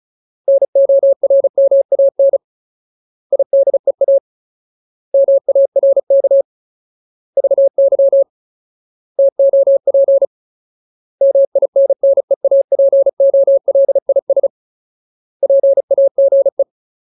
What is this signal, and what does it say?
Morse code, heard as a keyed tone, about 34 wpm, 550 Hz. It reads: NORMAN IDEA MARK VY TOP MINNEAPOLIS PAGE
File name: Ultimate Word List / 35WPM Ultimate